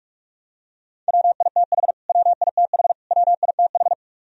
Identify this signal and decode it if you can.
Morse code, heard as a keyed tone, about 44 wpm, 700 Hz. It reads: WITH WITH WITH
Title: Top 100 Words Repeated 45wpm